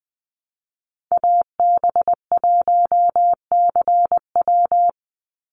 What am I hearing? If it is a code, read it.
Morse code, heard as a keyed tone, about 20 wpm, 700 Hz. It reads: AB1CW
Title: Call Signs Repeated 3x - 20wpm